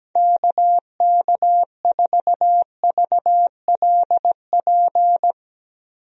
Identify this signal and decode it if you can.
Morse code, heard as a keyed tone, about 17 wpm, 700 Hz. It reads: KK4VLP
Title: USA Call Signs - 6 Characters - Repeated 3x - 17wpm